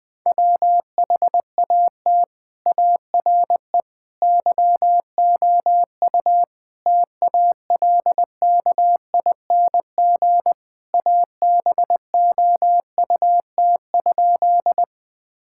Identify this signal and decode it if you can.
Morse code, heard as a keyed tone, about 21 wpm, 700 Hz. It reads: WHAT ARE YOU TALKING ABOUT?
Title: Sentences with 5-Word Phrases - Spoken Beginning - 20wpm